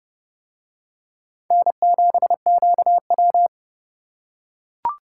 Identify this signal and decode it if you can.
Morse code, heard as a keyed tone, about 32 wpm, 700 Hz. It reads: N7QW